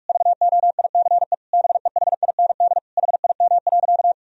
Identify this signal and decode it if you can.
Morse code, heard as a keyed tone, about 46 wpm, 700 Hz. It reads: VOICE BEHIND HIM.
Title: Sentences from Top 1000 Words - Review Part 1 - 45wpm